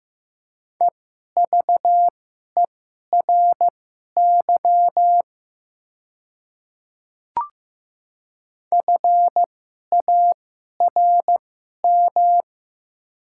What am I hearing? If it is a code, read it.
Morse code, heard as a keyed tone, about 15 wpm, spaced as for 11 wpm, 700 Hz. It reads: EVERY FARM
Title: 2x Character Spacing - Top 500 Words - 15wpm